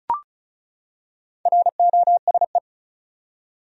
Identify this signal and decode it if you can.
Morse code, heard as a keyed tone, about 34 wpm, 700 Hz. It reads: ROSE
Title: USA Common Names - 35wpm